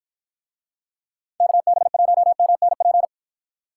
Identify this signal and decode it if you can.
Morse code, heard as a keyed tone, about 50 wpm, 700 Hz. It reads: KB1MNP